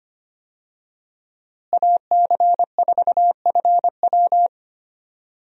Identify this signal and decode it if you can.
Morse code, heard as a keyed tone, about 25 wpm, 700 Hz. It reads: AC4FW
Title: USA Call Signs - 5 Characters - Repeated 3x - 25wpm